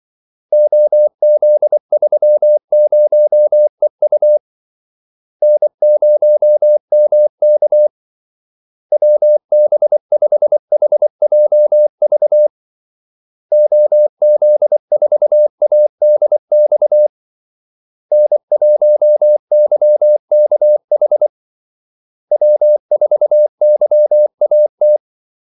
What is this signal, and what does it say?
Morse code, heard as a keyed tone, about 24 wpm, 600 Hz. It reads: OZ30EU N0MK WB5HJV OZ4ADX N1YKH W4YAT
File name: Callsigns-24wpm